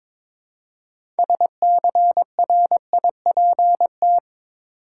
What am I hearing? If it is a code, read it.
Morse code, heard as a keyed tone, about 22 wpm, 700 Hz. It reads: SCRIPT
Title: Six Letter Words - 22wpm